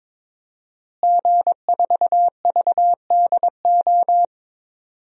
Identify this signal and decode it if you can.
Morse code, heard as a keyed tone, about 22 wpm, 700 Hz. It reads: G4VDO
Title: DXCC-100 Call Signs - 5 Characters - Repeated 3x - 22wpm